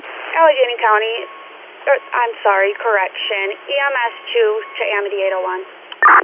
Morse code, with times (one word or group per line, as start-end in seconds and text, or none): none